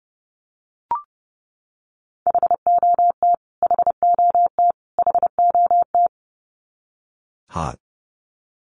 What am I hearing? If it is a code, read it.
Morse code, heard as a keyed tone, about 30 wpm, 700 Hz. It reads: HOT HOT HOT
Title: Top 500 Words Repeated 30wpm